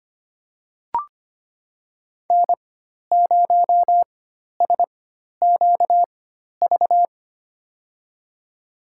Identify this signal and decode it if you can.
Morse code, heard as a keyed tone, about 25 wpm, spaced as for 12 wpm, 700 Hz. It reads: N0SQV
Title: USA Call Signs - 5 Characters - 4x Character Spacing - 25wpm